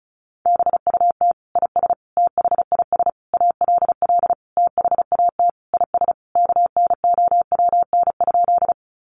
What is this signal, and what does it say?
Morse code, heard as a keyed tone, about 34 wpm, 700 Hz. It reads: BUT IS THIS ALL THAT IS KNOWN?
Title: Sentences from Top 600 Words 35wpm